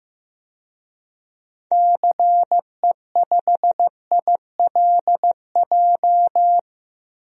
Morse code, heard as a keyed tone, about 15 wpm, 700 Hz. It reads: CE5ILJ
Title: DXCC-100 Call Signs - 6 Characters - 15wpm